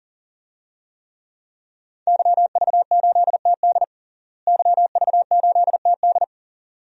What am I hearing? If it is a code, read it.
Morse code, heard as a keyed tone, about 40 wpm, 700 Hz. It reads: YV8TD YV8TD
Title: DXCC-100 Call Signs - 5 Characters - Repeated 2x - 40wpm